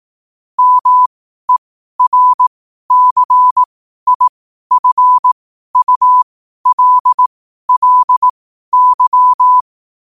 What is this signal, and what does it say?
Morse code, heard as a keyed tone, about 18 wpm, spaced as for 12 wpm, 1000 Hz. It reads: MERCIFULLY
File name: qotdfiles / 12wpm